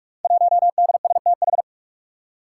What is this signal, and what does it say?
Morse code, heard as a keyed tone, about 44 wpm, 700 Hz. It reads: 1DSTH
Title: Five Challenging Letter-Number Combinations 45wpm